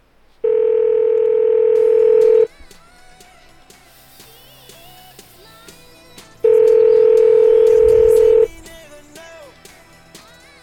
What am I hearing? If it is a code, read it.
Ringback tone (North America), 440 + 480 Hz.